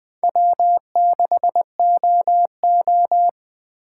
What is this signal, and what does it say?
Morse code, heard as a keyed tone, about 20 wpm, 700 Hz. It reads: W6OO
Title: USA Call Signs - 4 Characters - 3x Character Spacing - 20wpm